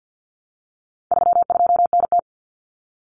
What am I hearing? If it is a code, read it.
Morse code, heard as a keyed tone, about 49 wpm, 700 Hz. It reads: 32NT
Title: Common Cut Numbers - 4-Digit 50wpm